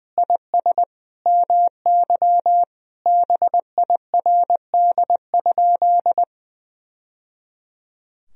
Morse code, heard as a keyed tone, about 20 wpm, 700 Hz. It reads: IS MY BIRD?
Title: Sentences from Top 1000 Words - Review Part 2 - 20wpm